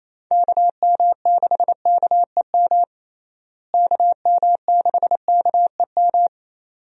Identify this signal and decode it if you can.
Morse code, heard as a keyed tone, about 28 wpm, 700 Hz. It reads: KM6KEM KM6KEM
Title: USA Call Signs - 6 Characters - Repeated 2x - 28wpm